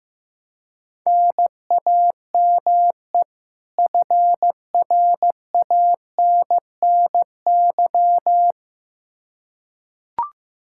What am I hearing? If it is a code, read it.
Morse code, heard as a keyed tone, about 15 wpm, 700 Hz. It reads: NAME FRANNY
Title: Practice QSO - Exchange 1 - Beginning - 15wpm